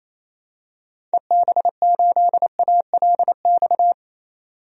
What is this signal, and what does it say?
Morse code, heard as a keyed tone, about 28 wpm, 700 Hz. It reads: EB8ALX
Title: DXCC-100 Call Signs - 6 Characters - 28wpm